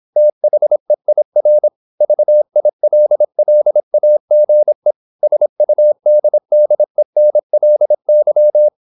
Morse code, heard as a keyed tone, about 26 wpm, 600 Hz. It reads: THEIR VILLAGE SUDDENLY
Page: Never 26wpm | CW med Gnister